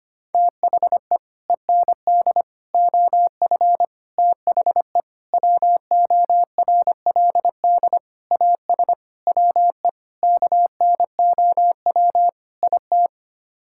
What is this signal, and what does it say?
Morse code, heard as a keyed tone, about 25 wpm, 700 Hz. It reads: THE END OF THE WORLD AS WE KNOW IT
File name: Sets of 9 Words - 25wpm